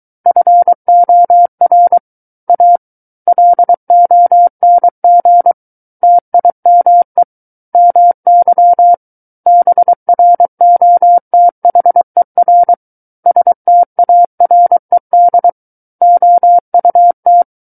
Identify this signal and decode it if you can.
Morse code, heard as a keyed tone, about 23 wpm, 700 Hz. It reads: FOR A LONG TIME MY BROTHER STARED OUT
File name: Book One / 14-Chapter 14 - 23 WPM